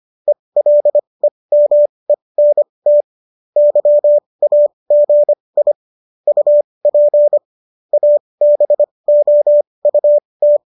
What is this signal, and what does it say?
Morse code, heard as a keyed tone, about 25 wpm, spaced as for 20 wpm, 590 Hz. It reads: ELEMENT YAGI UP ABOUT